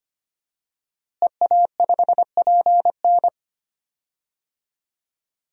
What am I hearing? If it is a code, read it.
Morse code, heard as a keyed tone, about 25 wpm, 700 Hz. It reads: EA5PN
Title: DXCC-100 Call Signs - 5 Characters - Repeated 2x - 25wpm